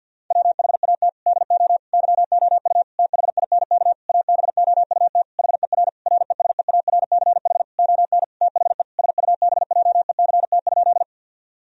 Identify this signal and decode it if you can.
Morse code, heard as a keyed tone, about 50 wpm, 700 Hz. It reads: WHAT DO YOU THINK ABOUT HER RESEARCH ON THE SUBJECT?
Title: Sentences from Top 1000 Words - Review Part 4 - 50wpm